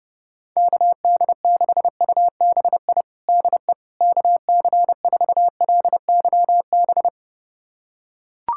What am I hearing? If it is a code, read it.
Morse code, heard as a keyed tone, about 30 wpm, 700 Hz. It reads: KD6UBI DE KC4LYB
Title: USA Callsign DE Callsign - 6 Characters - 30wpm